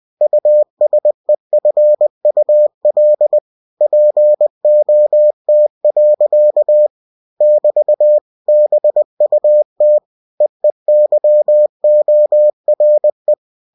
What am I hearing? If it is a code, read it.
Morse code, heard as a keyed tone, about 20 wpm, 600 Hz. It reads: USEFUL POT. <BT> BUT EEYORE
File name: cw-books / winnie-the-pooh / winnie-20wpm-page-0058